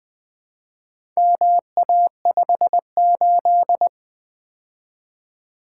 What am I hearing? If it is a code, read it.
Morse code, heard as a keyed tone, about 20 wpm, 700 Hz. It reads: MA58
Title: Four Letter-Number Combinations - 20wpm